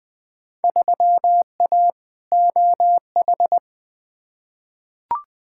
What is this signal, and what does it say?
Morse code, heard as a keyed tone, about 20 wpm, 700 Hz. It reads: 3A OH
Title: Field Day Exchanges - 20wpm